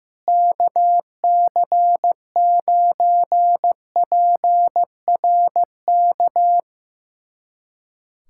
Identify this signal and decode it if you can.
Morse code, heard as a keyed tone, about 15 wpm, 700 Hz. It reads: KC9PRK
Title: USA Callsign DE Callsign - 6 Characters - 15wpm